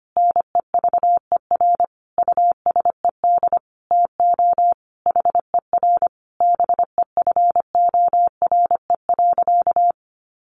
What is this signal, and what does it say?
Morse code, heard as a keyed tone, about 25 wpm, 700 Hz. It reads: NEVER USED TO HER BEFORE.
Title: Sentences from Top 600 Words 25wpm